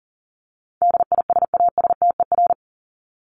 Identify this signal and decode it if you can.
Morse code, heard as a keyed tone, about 40 wpm, 700 Hz. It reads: DISASTER
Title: Top 1501-1750 Words 40wpm